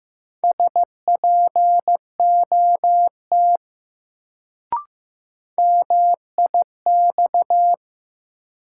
Morse code, heard as a keyed tone, about 15 wpm, 700 Hz. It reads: SPOT MIX
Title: Sentences from Top 1100 Words - New Word Drill Alt - 15wpm